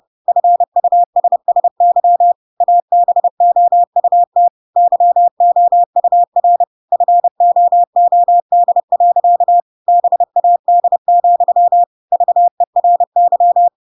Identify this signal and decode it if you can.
Morse code, heard as a keyed tone, about 30 wpm, 700 Hz. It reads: FUSSY ABOUT YOUR FOOD. BAD, VERY
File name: Pinocchio / Pinocchio ch7 30 30 ik1hge